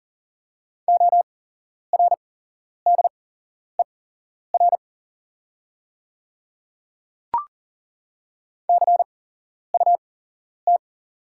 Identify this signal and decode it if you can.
Morse code, heard as a keyed tone, about 40 wpm, spaced as for 11 wpm, 690 Hz. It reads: ORDER CUT
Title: 8x Character Spacing - Top 500 Words - 40wpm